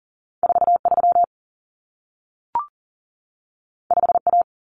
Morse code, heard as a keyed tone, about 40 wpm, 700 Hz. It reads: 43 5A
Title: Common Cut Numbers - 2-Digit 40wpm